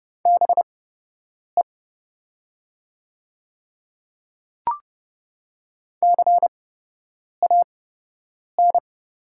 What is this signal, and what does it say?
Morse code, heard as a keyed tone, about 29 wpm, spaced as for 8 wpm, 700 Hz. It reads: BE CAN
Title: 8x Character Spacing - Top 100 Words - 30wpm